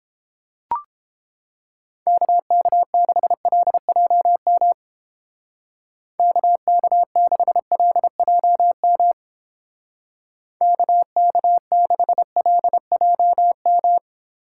Morse code, heard as a keyed tone, about 29 wpm, 700 Hz. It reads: KK6LJM KK6LJM KK6LJM